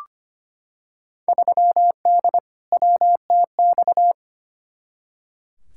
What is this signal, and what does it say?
Morse code, heard as a keyed tone, about 25 wpm, 700 Hz. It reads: 3D WTX
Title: Field Day Exchanges Encore - 25wpm